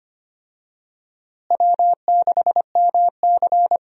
Morse code, heard as a keyed tone, about 25 wpm, 700 Hz. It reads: W6MC
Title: USA Call Signs - 4 Characters - Repeated 2x - 25wpm